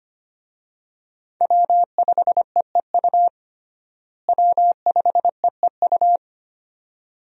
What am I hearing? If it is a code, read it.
Morse code, heard as a keyed tone, about 25 wpm, 700 Hz. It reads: W5EEU W5EEU
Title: USA Call Signs - 5 Characters - Repeated 2x - 25wpm